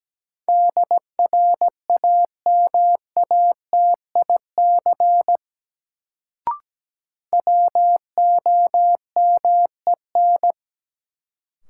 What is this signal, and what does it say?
Morse code, heard as a keyed tone, about 17 wpm, 700 Hz. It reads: DRAMATIC WOMEN
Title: Sentences from Top 1300 Words - New Word Drill - 17wpm